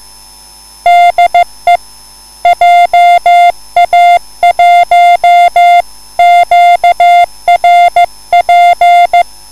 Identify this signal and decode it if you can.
Morse code, heard as a keyed tone, about 15 wpm, 700 Hz. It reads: DE JA1QRP
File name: QSL / CW lesson